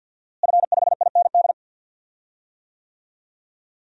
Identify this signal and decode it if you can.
Morse code, heard as a keyed tone, about 45 wpm, 700 Hz. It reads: F5IND
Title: DXCC-100 Call Signs - 5 Characters - Repeated 2x - 50wpm